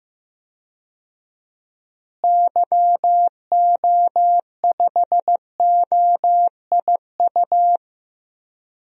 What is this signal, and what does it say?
Morse code, heard as a keyed tone, about 15 wpm, 700 Hz. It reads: YO5OIU
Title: DXCC-100 Call Signs - 6 Characters - Repeated 3x - 15wpm